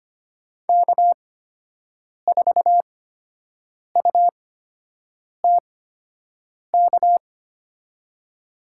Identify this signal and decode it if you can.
Morse code, heard as a keyed tone, about 25 wpm, spaced as for 7 wpm, 700 Hz. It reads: K4UTK
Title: USA Call Signs - 5 Characters - 8x Character Spacing - 25wpm